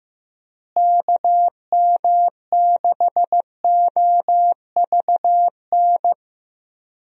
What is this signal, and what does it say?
Morse code, heard as a keyed tone, about 15 wpm, 700 Hz. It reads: KM6OVN
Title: USA Call Signs - 6 Characters - Repeated 2x - 15wpm